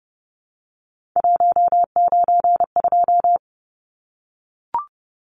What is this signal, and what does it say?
Morse code, heard as a keyed tone, about 30 wpm, 700 Hz. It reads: 192
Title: Numbers - 3-Digit 30wpm